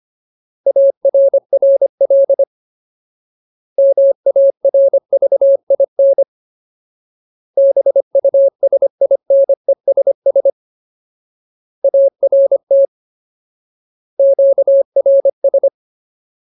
Morse code, heard as a keyed tone, about 25 wpm, 550 Hz. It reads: ARRL MARVIN BUSINESS ART QRS
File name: Ultimate Word List / 25WPM Ultimate